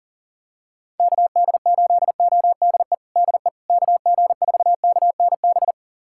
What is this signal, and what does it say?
Morse code, heard as a keyed tone, about 40 wpm, 700 Hz. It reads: KD8ODE DE KG4KNB
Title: USA Callsign DE Callsign - 6 Characters - 40wpm